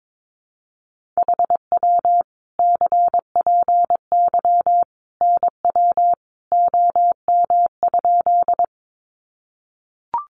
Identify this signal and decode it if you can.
Morse code, heard as a keyed tone, about 22 wpm, 700 Hz. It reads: HW CPY NW OM?
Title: Common QSO Phrases 22wpm